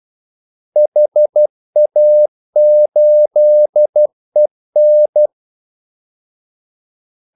Morse code, heard as a keyed tone, about 12 wpm, 600 Hz. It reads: HA8EN
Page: 2 ekstra ordmellemrum mellem kaldesignalerne